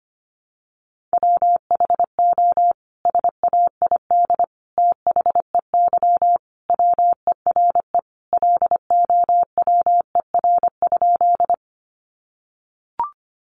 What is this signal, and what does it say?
Morse code, heard as a keyed tone, about 25 wpm, 700 Hz. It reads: WHO SAID THEY WERE LOWER?
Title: Sentences from Top 700 Words 25wpm